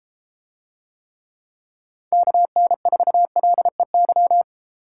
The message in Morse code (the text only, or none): KN4LEY